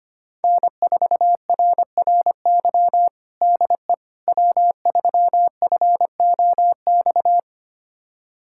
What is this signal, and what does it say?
Morse code, heard as a keyed tone, about 25 wpm, 700 Hz. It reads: N4RRY DE W3FOX
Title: USA Callsign DE Callsign - 5 Characters - 25wpm